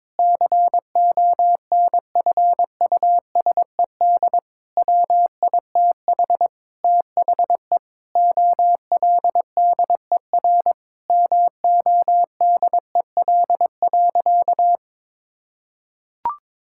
Morse code, heard as a keyed tone, about 22 wpm, 700 Hz. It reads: CONFUSED WITH THE OLDER MODEL.
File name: Sentences with 7-Word Phrases - 22wpm